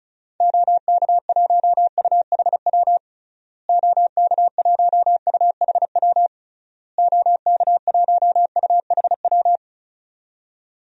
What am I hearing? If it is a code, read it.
Morse code, heard as a keyed tone, about 34 wpm, 700 Hz. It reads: OK1UHW OK1UHW OK1UHW
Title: DXCC-100 Call Signs - 6 Characters - Repeated 3x - 35wpm